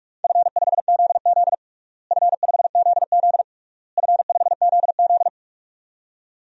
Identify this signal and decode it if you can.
Morse code, heard as a keyed tone, about 46 wpm, 700 Hz. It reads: F5ZZ F5ZZ F5ZZ
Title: DXCC-100 Call Signs - 4 Characters - Repeated 3x - 45wpm